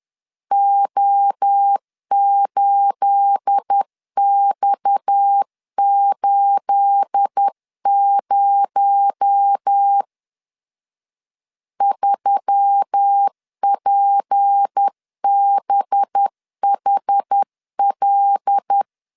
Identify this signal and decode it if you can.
Morse code, heard as a keyed tone, about 10 wpm, 800 Hz. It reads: O8X80 3PBHL
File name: LEZIONI / 9 28 10wpm 2842